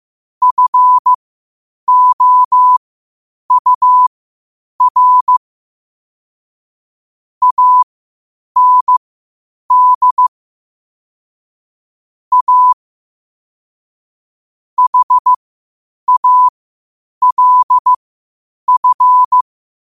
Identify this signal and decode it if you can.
Morse code, heard as a keyed tone, about 15 wpm, spaced as for 8 wpm, 1000 Hz. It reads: FOUR AND A HALF
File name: qotdfiles / 8wpm